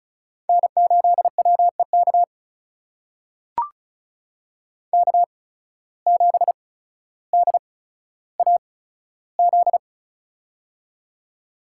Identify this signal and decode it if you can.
Morse code, heard as a keyed tone, about 36 wpm, spaced as for 15 wpm, 700 Hz. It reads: N8WEK K7DAZ
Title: USA Call Signs - 5 Characters - 8x Character Spacing - 35wpm